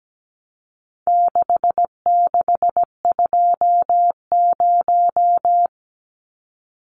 Morse code, heard as a keyed tone, about 17 wpm, 700 Hz. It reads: 6620
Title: Numbers - 4-Digit 17wpm